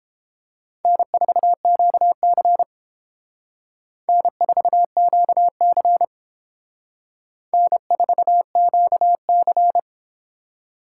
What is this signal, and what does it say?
Morse code, heard as a keyed tone, about 29 wpm, 700 Hz. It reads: N4QC N4QC N4QC